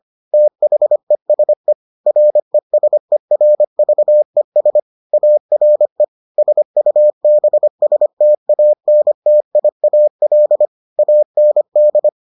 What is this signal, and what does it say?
Morse code, heard as a keyed tone, about 25 wpm, 600 Hz. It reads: THESE RESERVES ARE SUBSTANTIAL AND